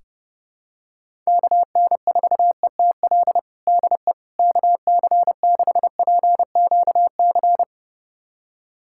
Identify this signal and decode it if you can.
Morse code, heard as a keyed tone, about 30 wpm, 700 Hz. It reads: KN4ETL DE KC6PQC
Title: USA Callsign DE Callsign 30wpm